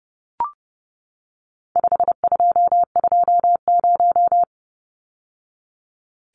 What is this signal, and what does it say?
Morse code, heard as a keyed tone, about 30 wpm, 700 Hz. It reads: 5220
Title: Numbers - 4-Digit 30wpm